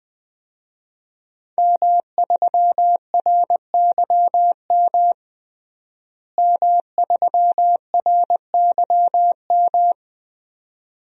Morse code, heard as a keyed tone, about 21 wpm, 700 Hz. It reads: M3RYM M3RYM
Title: DXCC-100 Call Signs - 5 Characters - Repeated 2x - 20wpm